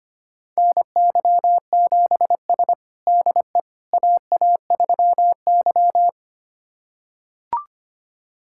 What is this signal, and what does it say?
Morse code, heard as a keyed tone, about 25 wpm, 700 Hz. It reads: NY7S DE AA3Y
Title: USA Callsign DE Callsign - 4 Characters - 25wpm